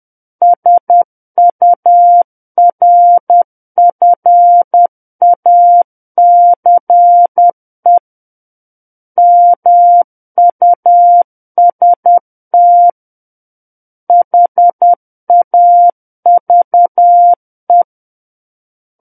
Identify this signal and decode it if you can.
Morse code, heard as a keyed tone, about 10 wpm, 700 Hz. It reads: SURFACE MUST HAVE
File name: Book One / 01-Chapter 1 - 10 WPM